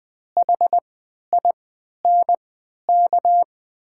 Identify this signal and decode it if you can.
Morse code, heard as a keyed tone, about 20 wpm, spaced as for 11 wpm, 700 Hz. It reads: HINK
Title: 3x Character Spacing - Top 500 Words - 20wpm